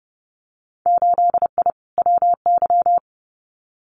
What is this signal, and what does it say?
Morse code, heard as a keyed tone, about 30 wpm, 700 Hz. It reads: 8I WY
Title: Winter Field Day Exchanges Encore 30wpm